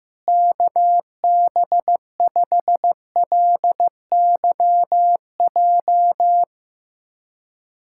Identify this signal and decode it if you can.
Morse code, heard as a keyed tone, about 15 wpm, 700 Hz. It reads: KB5LYJ
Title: USA Callsign DE Callsign 15wpm